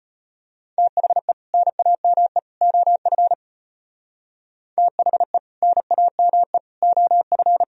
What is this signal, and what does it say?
Morse code, heard as a keyed tone, about 33 wpm, 700 Hz. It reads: THE NAME OF THE NAME OF